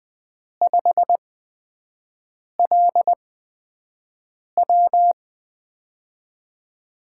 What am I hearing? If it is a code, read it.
Morse code, heard as a keyed tone, about 20 wpm, spaced as for 5 wpm, 700 Hz. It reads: 5LW
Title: USA Call Signs - 5 Characters - 8x Character Spacing - 20wpm